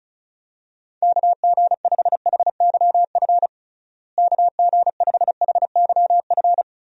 Morse code, heard as a keyed tone, about 34 wpm, 700 Hz. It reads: KG5HYF KG5HYF
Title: USA Call Signs - 6 Characters - Repeated 3x - 35wpm